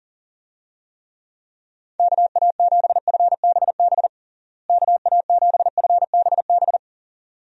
Morse code, heard as a keyed tone, about 40 wpm, 700 Hz. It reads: KA7FBB KA7FBB
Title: USA Call Signs - 6 Characters - Repeated 2x - 40wpm